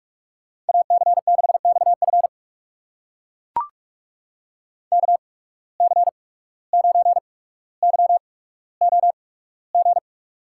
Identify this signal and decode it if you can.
Morse code, heard as a keyed tone, about 44 wpm, spaced as for 18 wpm, 700 Hz. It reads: AC6XF KC9YOG
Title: USA Call Signs - 8x Character Spacing - 45wpm